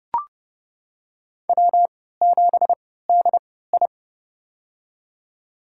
Morse code, heard as a keyed tone, about 30 wpm, spaced as for 17 wpm, 700 Hz. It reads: W7DI
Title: USA Call Signs - 4 Characters - 3x Character Spacing - 30wpm